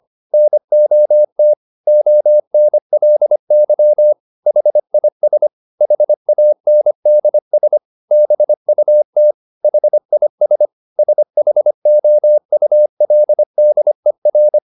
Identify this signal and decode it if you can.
Morse code, heard as a keyed tone, about 25 wpm, 600 Hz. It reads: NOT ONLY HIS HANDS BUT HIS SHOULDER